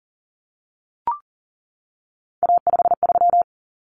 Morse code, heard as a keyed tone, about 40 wpm, 700 Hz. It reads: A53
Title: Common Cut Numbers - 3-Digit 40wpm